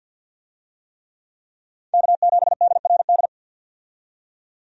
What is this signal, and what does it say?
Morse code, heard as a keyed tone, about 46 wpm, 700 Hz. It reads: K7DRD
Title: USA Call Signs - 5 Characters - 50wpm